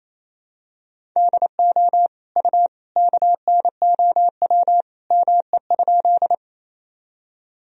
Morse code, heard as a keyed tone, about 28 wpm, 700 Hz. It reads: DO U KNOW ME?
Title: Sentences from Top 200 Words and Abbreviations 28wpm